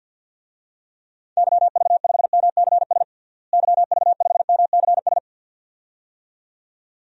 Morse code, heard as a keyed tone, about 50 wpm, 700 Hz. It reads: YV5MCS YV5MCS
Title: DXCC-100 Call Signs - 6 Characters - Repeated 2x - 50wpm